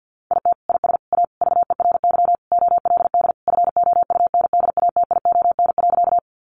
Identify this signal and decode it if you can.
Morse code, heard as a keyed tone, about 50 wpm, 700 Hz. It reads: IT IS A VERY OLD FOUNDATION.
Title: Sentences from Top 1200 Words 50wpm